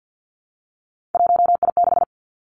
Morse code, heard as a keyed tone, about 50 wpm, 700 Hz. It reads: 1I6
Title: Three Challenging Letter-Number Combinations 50wpm